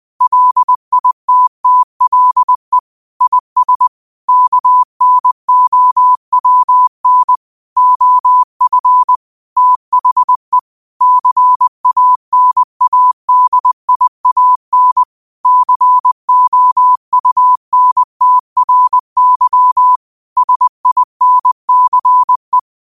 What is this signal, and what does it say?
Morse code, heard as a keyed tone, about 20 wpm, 1000 Hz. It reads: LITTLE IS KNOWN OF THE CANADIAN COUNTRY SINCE
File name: qotdfiles / 20wpm